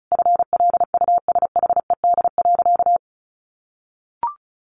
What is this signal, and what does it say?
Morse code, heard as a keyed tone, about 34 wpm, 700 Hz. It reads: FLUSHED.
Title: A Sound Of Thunder 35wpm